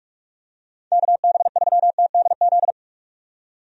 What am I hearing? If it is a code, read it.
Morse code, heard as a keyed tone, about 42 wpm, 700 Hz. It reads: KB3TDZ
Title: USA Call Signs - 6 Characters - 45wpm